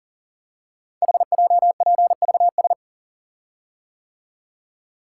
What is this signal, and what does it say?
Morse code, heard as a keyed tone, about 38 wpm, 700 Hz. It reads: HJPVS
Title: Code Groups - 5 Letters - 40wpm